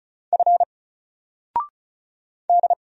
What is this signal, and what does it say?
Morse code, heard as a keyed tone, about 34 wpm, 700 Hz. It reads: F D
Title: Single Character (Letter:Number:Punctuation:Prosign) 35wpm